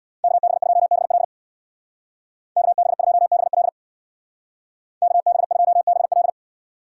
Morse code, heard as a keyed tone, about 67 wpm, 700 Hz. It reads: KB2BL KB2BL KB2BL